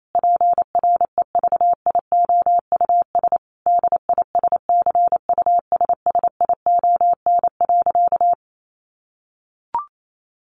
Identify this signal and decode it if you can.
Morse code, heard as a keyed tone, about 28 wpm, 700 Hz. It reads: PREVIOUS DISCUSSION.
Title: Sentences from Top 1500 Words 28wpm